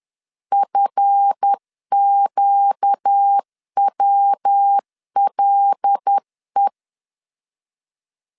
Morse code, heard as a keyed tone, about 10 wpm, 800 Hz. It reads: FQWLE
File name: LEZIONI / 7 GF9Z / 7 GF 10wpm 2514